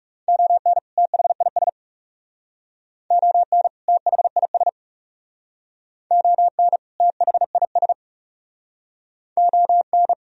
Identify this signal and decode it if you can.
Morse code, heard as a keyed tone, about 39 wpm, 700 Hz. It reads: ON THIS ON THIS ON THIS ON